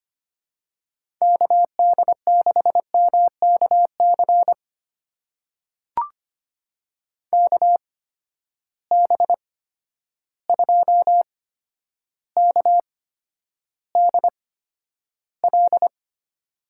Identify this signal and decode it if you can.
Morse code, heard as a keyed tone, about 25 wpm, spaced as for 11 wpm, 700 Hz. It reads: KD6MKC KB2KDL
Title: USA Call Signs - 6 Characters - 8x Character Spacing - 25wpm